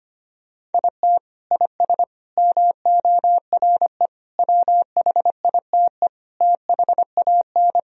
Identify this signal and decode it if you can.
Morse code, heard as a keyed tone, about 25 wpm, 700 Hz. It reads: IT IS MORE WHITE THAN
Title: Sentences from Top 1000 Words - Review Part 1 - 25wpm